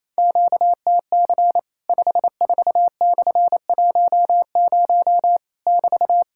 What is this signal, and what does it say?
Morse code, heard as a keyed tone, about 27 wpm, 700 Hz. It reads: QTC 54/10 <BT>
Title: Worked All Europe DX Contest - QTC Exchange - 28wpm